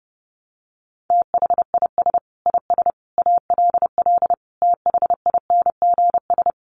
Morse code, heard as a keyed tone, about 30 wpm, 700 Hz. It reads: THIS IS ALL THINGS
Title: Sets of 5 Words 30wpm